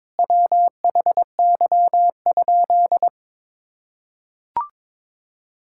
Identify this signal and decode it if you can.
Morse code, heard as a keyed tone, about 22 wpm, 700 Hz. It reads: WHY?
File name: practice / 22wpm / Sentences with 4-Word Phrases - 22wpm